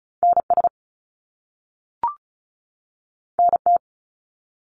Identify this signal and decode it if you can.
Morse code, heard as a keyed tone, about 36 wpm, 700 Hz. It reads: NS NT
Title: Canadian Provinces and Territories 35wpm